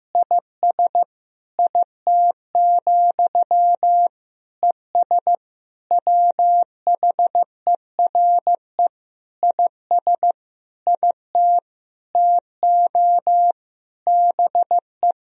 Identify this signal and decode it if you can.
Morse code, heard as a keyed tone, about 15 wpm, 700 Hz. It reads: IS IT, ES WHERE IS IT TO BE
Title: Sentences from Top 300 Words and Abbreviations 15wpm